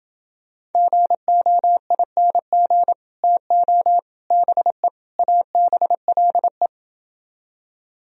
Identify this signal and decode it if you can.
Morse code, heard as a keyed tone, about 30 wpm, 700 Hz. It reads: GOING TO BE ABLE